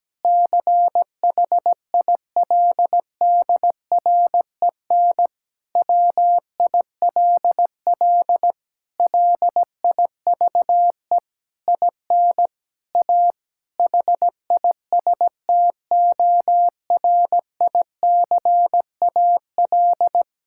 Morse code, heard as a keyed tone, about 17 wpm, 700 Hz. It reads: CHILDREN WILL LIVE IN A HISTORICAL
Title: Sentences from Top 1000 Words - Review Part 5 - 17wpm